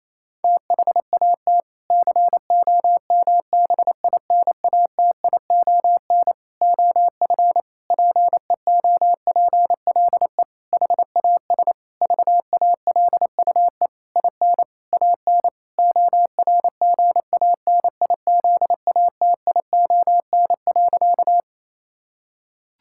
Morse code, heard as a keyed tone, about 28 wpm, 700 Hz. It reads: THAT COMBINATION OF PEOPLE HAS VALUE IN AN ORGANIZATION.
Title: Sentences from Top 1000 Words - Review Part 6 - 28wpm